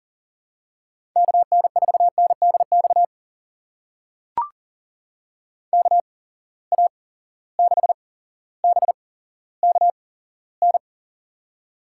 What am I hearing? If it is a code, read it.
Morse code, heard as a keyed tone, about 40 wpm, spaced as for 17 wpm, 700 Hz. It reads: KN4NDX KA6BKN
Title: USA Call Signs - 8x Character Spacing - 40wpm